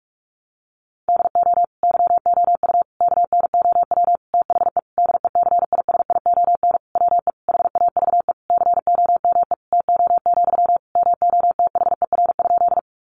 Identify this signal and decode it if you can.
Morse code, heard as a keyed tone, about 44 wpm, 700 Hz. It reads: DO YOU KNOW THE DECISION WE HAVE COME TO, MOTHER?
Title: Sentences from Top 1000 Words 45wpm